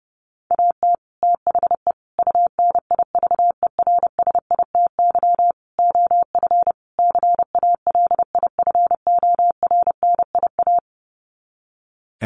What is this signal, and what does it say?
Morse code, heard as a keyed tone, about 28 wpm, 700 Hz. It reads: AT THE UNIVERSITY OF CALIFORNIA
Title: Sets of 5 Words Encore2 30wpm